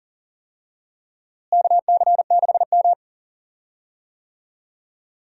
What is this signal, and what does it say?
Morse code, heard as a keyed tone, about 40 wpm, 700 Hz. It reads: KC6M
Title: USA Call Signs - 4 Characters - Repeated 3x - 40wpm